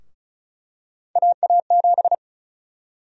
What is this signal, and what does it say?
Morse code, heard as a keyed tone, about 36 wpm, 700 Hz. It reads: AA7
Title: USA Call Sign Prefixes - 35wpm